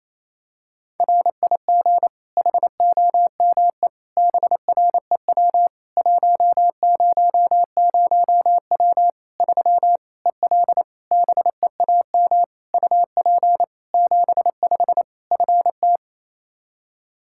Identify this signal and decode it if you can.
Morse code, heard as a keyed tone, about 28 wpm, 700 Hz. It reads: RIG HOME BREW 100W 3 EL BEAM UP 75 FT
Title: K1USN SST QSO Practice 28wpm